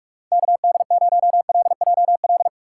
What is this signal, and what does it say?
Morse code, heard as a keyed tone, about 46 wpm, 700 Hz. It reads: KD0LJL
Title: USA Call Signs - 2x Character Spacing - 45wpm